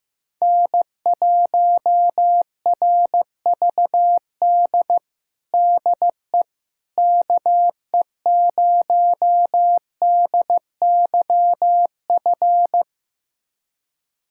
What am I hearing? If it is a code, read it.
Morse code, heard as a keyed tone, about 15 wpm, 700 Hz. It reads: N1RVD DE KE0DYF
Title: USA Callsign DE Callsign 15wpm